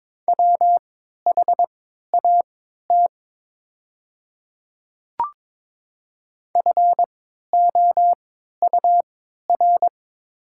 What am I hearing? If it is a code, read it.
Morse code, heard as a keyed tone, about 22 wpm, spaced as for 12 wpm, 700 Hz. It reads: WHAT FOUR
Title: 3x Character Spacing - Top 300 Words - 22wpm